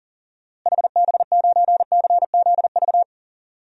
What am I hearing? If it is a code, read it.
Morse code, heard as a keyed tone, about 40 wpm, 700 Hz. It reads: HB9CZV
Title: DXCC-100 Call Signs - 6 Characters - Repeated 3x - 40wpm